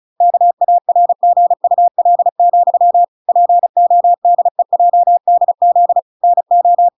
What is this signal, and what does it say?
Morse code, heard as a keyed tone, about 36 wpm, 700 Hz. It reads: KARGUL, PODEJDZ NO